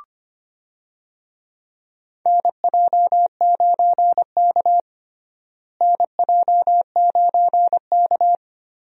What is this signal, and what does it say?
Morse code, heard as a keyed tone, about 25 wpm, 700 Hz. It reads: NJ9K NJ9K
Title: USA Call Signs - 4 Characters - Repeated 2x - 25wpm